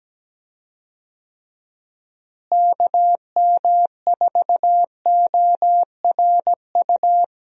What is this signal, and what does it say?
Morse code, heard as a keyed tone, about 17 wpm, 700 Hz. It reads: KM4ORU
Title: USA Call Signs - 6 Characters - Repeated 2x - 17wpm